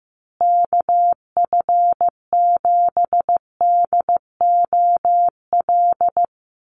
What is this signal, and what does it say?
Morse code, heard as a keyed tone, about 15 wpm, 700 Hz. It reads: KF7DOL
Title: USA Call Signs Repeated 2x 15wpm